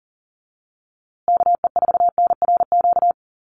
Morse code, heard as a keyed tone, about 40 wpm, 700 Hz. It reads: KE4NRQ
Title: USA Call Signs Repeated 2x 40wpm